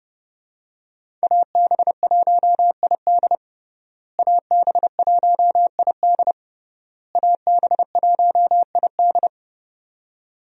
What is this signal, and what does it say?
Morse code, heard as a keyed tone, about 30 wpm, 700 Hz. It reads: AB1ID AB1ID AB1ID
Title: USA Call Signs - 5 Characters - Repeated 3x - 30wpm